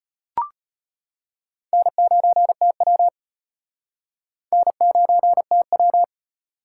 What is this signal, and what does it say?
Morse code, heard as a keyed tone, about 31 wpm, 700 Hz. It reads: N9TW N9TW
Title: Speed Racing - USA 1x2 Call Signs 25wpm